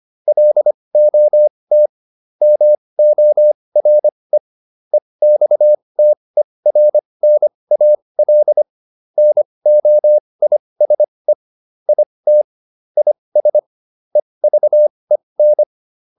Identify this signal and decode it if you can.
Morse code, heard as a keyed tone, about 25 wpm, spaced as for 20 wpm, 600 Hz. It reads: LOT MORE EXTERNAL NOISE IT IS EVEN